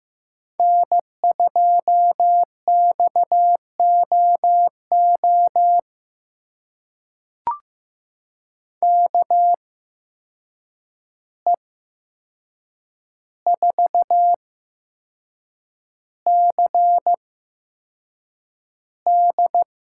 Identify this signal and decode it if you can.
Morse code, heard as a keyed tone, about 15 wpm, spaced as for 6 wpm, 700 Hz. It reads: N2XOO KE4CD
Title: USA Call Signs - 5 Characters - 8x Character Spacing - 15wpm